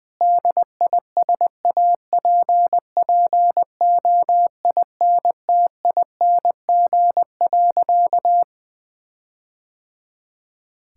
Morse code, heard as a keyed tone, about 20 wpm, 700 Hz. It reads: DISAPPOINTING.
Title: Sentences with 4-Word Phrases - Spoken Beginning - 20wpm